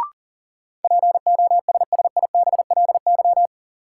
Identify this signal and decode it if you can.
Morse code, heard as a keyed tone, about 37 wpm, 700 Hz. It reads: POSSIBLY
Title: Sentences from Top 1300 Words - New Word Drill Alt - 40wpm